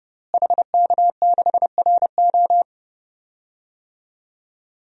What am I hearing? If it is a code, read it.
Morse code, heard as a keyed tone, about 30 wpm, 700 Hz. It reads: HK6RO
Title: DXCC-100 Call Signs - 5 Characters - Repeated 3x - 30wpm